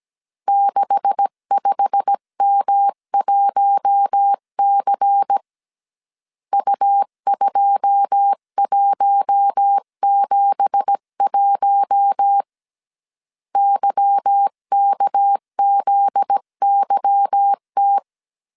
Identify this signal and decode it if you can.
Morse code, heard as a keyed tone, about 17 wpm, 800 Hz. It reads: 65M1C U2171 YKZYT
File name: CW / DET / 1 9 1500C S B 16wpm 2856